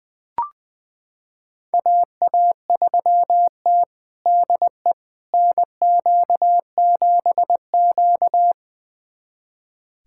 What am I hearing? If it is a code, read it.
Morse code, heard as a keyed tone, about 21 wpm, 700 Hz. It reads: AA3T DE NQ7Q
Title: USA Callsign DE Callsign - 4 Characters - 20wpm